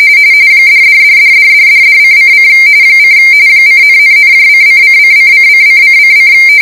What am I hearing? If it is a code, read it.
RTTY, about 45.45 baud. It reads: $$$$$$$$::: : :,,,:::::::::::